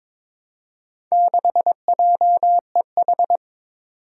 Morse code, heard as a keyed tone, about 22 wpm, 700 Hz. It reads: 6JEH
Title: Four Challenging Letter-Number Combinations 22wpm